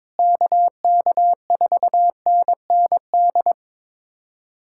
Morse code, heard as a keyed tone, about 22 wpm, 700 Hz. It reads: KK4NND
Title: USA Callsign DE Callsign 22wpm